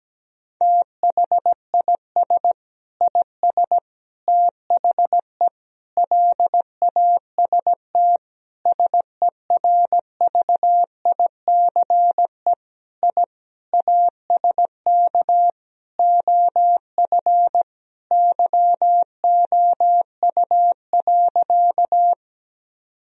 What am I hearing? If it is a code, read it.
Morse code, heard as a keyed tone, about 17 wpm, 700 Hz. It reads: THIS IS THE LAST SERVICE I ASK OF YOU.
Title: Sentences from Top 1000 Words - Review Part 4 - 17wpm